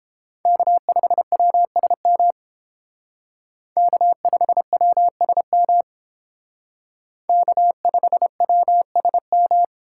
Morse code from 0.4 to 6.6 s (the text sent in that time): K5WSM K5WSM